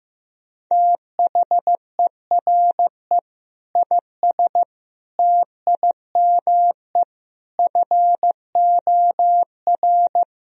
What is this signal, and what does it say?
Morse code, heard as a keyed tone, about 16 wpm, 700 Hz. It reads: THERE IS TIME FOR
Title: Sentences from Top 1000 Words - Review Part 3 - 15wpm